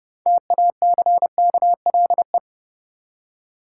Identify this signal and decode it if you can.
Morse code, heard as a keyed tone, about 32 wpm, 700 Hz. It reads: TACKLE
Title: Six Letter Words - 30wpm